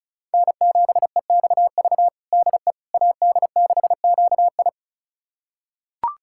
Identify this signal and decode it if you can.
Morse code, heard as a keyed tone, about 34 wpm, 690 Hz. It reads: N7EXV DE AD6QI
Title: USA Callsign DE Callsign - 5 Characters - 35wpm